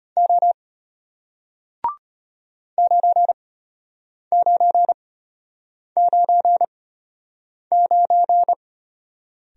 Morse code, heard as a keyed tone, about 33 wpm, spaced as for 8 wpm, 700 Hz. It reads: O 9999